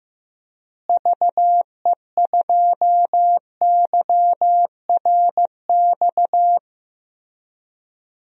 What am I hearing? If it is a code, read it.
Morse code, heard as a keyed tone, about 15 wpm, 700 Hz. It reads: VE2YRX
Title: DXCC-100 Call Signs - 6 Characters - Repeated 3x - 15wpm